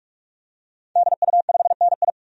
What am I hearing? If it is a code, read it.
Morse code, heard as a keyed tone, about 46 wpm, 700 Hz. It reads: DU5NI
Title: DXCC-100 Call Signs - 5 Characters - 45wpm